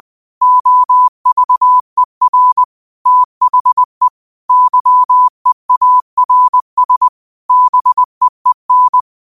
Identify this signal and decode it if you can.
Morse code, heard as a keyed tone, about 20 wpm, 1000 Hz. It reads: OVER THE YEARS BEEN